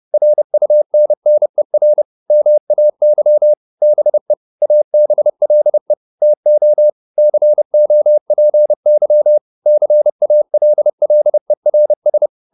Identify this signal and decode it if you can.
Morse code, heard as a keyed tone, about 30 wpm, 600 Hz. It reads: RUNNER MAY BE ABLE TO COPY CALLERS